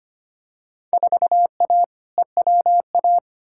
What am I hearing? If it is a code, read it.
Morse code, heard as a keyed tone, about 25 wpm, 700 Hz. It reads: 4A EWA
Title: Field Day Exchanges Encore - 25wpm